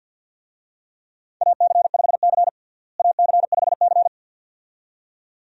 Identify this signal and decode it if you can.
Morse code, heard as a keyed tone, about 51 wpm, 700 Hz. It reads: AC5C AC5C
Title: USA Call Signs - 4 Characters - Repeated 2x - 50wpm